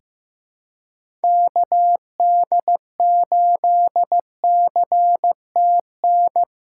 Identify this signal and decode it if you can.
Morse code, heard as a keyed tone, about 15 wpm, 700 Hz. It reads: KD8CTN
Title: USA Call Signs - 6 Characters - 4x Character Spacing - 15wpm